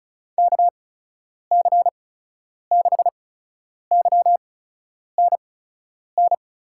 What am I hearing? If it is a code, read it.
Morse code, heard as a keyed tone, about 36 wpm, spaced as for 10 wpm, 700 Hz. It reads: KC6YNN
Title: USA Call Signs - 6 Characters - 8x Character Spacing - 35wpm